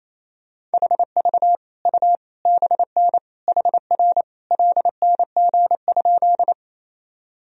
Morse code, heard as a keyed tone, about 27 wpm, 700 Hz. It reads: HV U BN HR LNG?
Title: Sentences from Top 200 Words and Abbreviations 28wpm